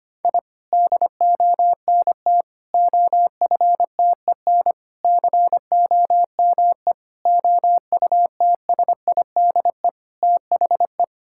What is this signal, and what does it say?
Morse code, heard as a keyed tone, about 25 wpm, 700 Hz. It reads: I DONT OFTEN COME OUTSIDE THE
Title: Sentences from Top 1000 Words - Review Part 6 - 25wpm